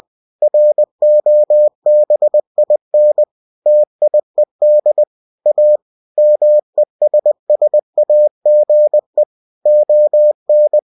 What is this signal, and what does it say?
Morse code, heard as a keyed tone, about 20 wpm, 600 Hz. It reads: ROBIN TIED A MESSAGE ON